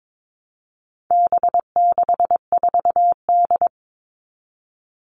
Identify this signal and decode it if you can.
Morse code, heard as a keyed tone, about 22 wpm, 700 Hz. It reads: B64D
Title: All Cut Numbers - 4-Digit 22wpm